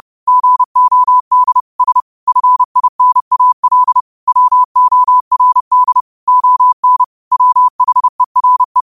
Morse code, heard as a keyed tone, about 30 wpm, 1000 Hz. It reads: GOD S FINAL WORD ON WHERE
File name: qotdfiles / 30wpm